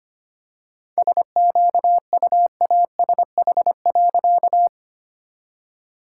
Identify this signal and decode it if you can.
Morse code, heard as a keyed tone, about 25 wpm, 700 Hz. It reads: SQUASH.
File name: practice / 25wpm / All-Age Appropriate Jokes - 25wpm